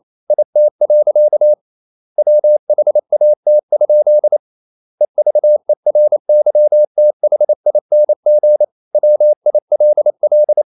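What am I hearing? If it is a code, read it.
Morse code, heard as a keyed tone, about 28 wpm, 600 Hz. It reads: IT. WHAT? EVERYTHING WILL
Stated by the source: Morse code practice at 28 wpm, text of Ken Follett's novel Never in English, from CW med Gnister.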